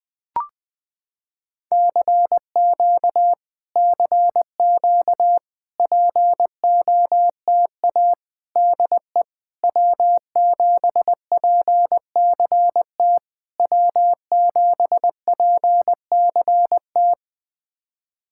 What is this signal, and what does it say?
Morse code, heard as a keyed tone, about 20 wpm, 700 Hz. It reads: CQ CQ POTA DE W7PCT W7PCT
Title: POTA QSO Practice - 20wpm